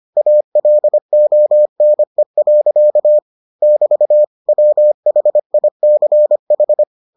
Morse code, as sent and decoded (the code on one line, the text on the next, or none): .- .-.. --- -. .-.-.- / -...- / .-- .... .. -.-. ....
ALONE. <BT> WHICH